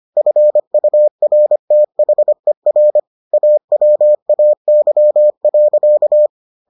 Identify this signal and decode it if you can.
Morse code, heard as a keyed tone, about 25 wpm, 600 Hz. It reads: FURTHER AWAY.